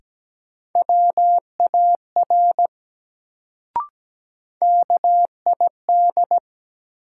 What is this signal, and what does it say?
Morse code, heard as a keyed tone, about 17 wpm, 700 Hz. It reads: WAR KID
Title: Sets of 5 Words - New Word Drill - 17wpm